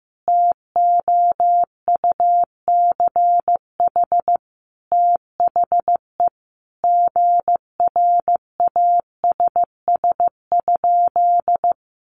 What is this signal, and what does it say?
Morse code, heard as a keyed tone, about 15 wpm, 700 Hz. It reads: TOUCH THE GRASS?
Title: A Sound Of Thunder 15wpm